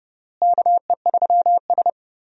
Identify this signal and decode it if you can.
Morse code, heard as a keyed tone, about 30 wpm, 700 Hz. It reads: KE3S
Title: USA Call Signs - 4 Characters - Repeated 2x - 30wpm